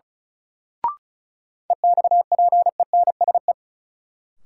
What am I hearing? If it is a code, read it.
Morse code, heard as a keyed tone, about 36 wpm, 700 Hz. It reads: EXPENSE
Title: Sets of 4 Words - New Word Drill - 35wpm